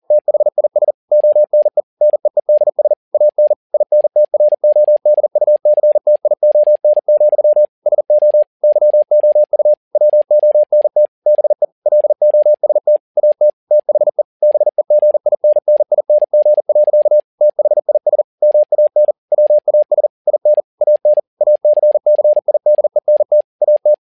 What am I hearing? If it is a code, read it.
Morse code, heard as a keyed tone, about 40 wpm, 600 Hz. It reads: THIS ONE NEEDS AN INTRODUCTION, SO YOU WONT BE LOST AT THE BEGINNING. THIS MAN WAS IN AN ACCIDENT AT
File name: learning center / tower trouble / tower touble 40wpm